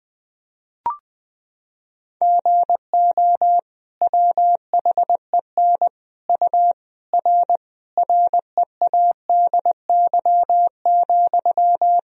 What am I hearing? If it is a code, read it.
Morse code, heard as a keyed tone, about 21 wpm, 700 Hz. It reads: GO WHEN U R READY,
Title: Sentences from Top 500 Words and Abbreviations 20wpm